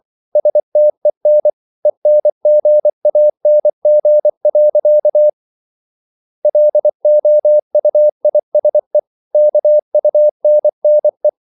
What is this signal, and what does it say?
Morse code, heard as a keyed tone, about 24 wpm, 600 Hz. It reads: STEN ENGANG. LOUISE KUNNE
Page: Ferie 24 wpm